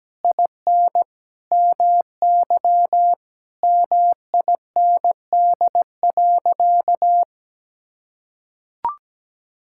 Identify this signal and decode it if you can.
Morse code, heard as a keyed tone, about 17 wpm, 700 Hz. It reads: IN MY MIND.
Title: Sentences from Top 1000 Words - Review Part 6 - 17wpm